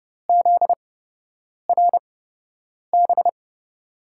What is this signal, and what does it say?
Morse code, heard as a keyed tone, about 31 wpm, spaced as for 8 wpm, 700 Hz. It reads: ZRB
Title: USA Call Signs - 6 Characters - 8x Character Spacing - 30wpm